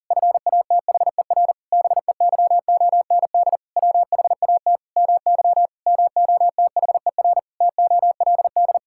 Morse code, heard as a keyed tone, about 40 wpm, 700 Hz. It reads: FATHER BEYOND WHAT MY MOTHER TOLD
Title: Sentences from Top 1000 Words - Review Part 6 - 40wpm